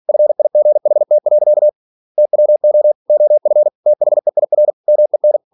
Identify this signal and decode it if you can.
Morse code, heard as a keyed tone, about 46 wpm, 600 Hz. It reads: FIGHT. TWO OF THEIR MEN
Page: Never 47wpm